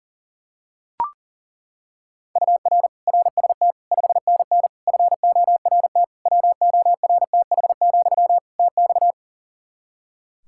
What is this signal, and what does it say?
Morse code, heard as a keyed tone, about 40 wpm, 700 Hz. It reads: UR RST 5NN FORT WORTH, TX
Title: Practice QSO - Exchange 1 - Beginning - 40wpm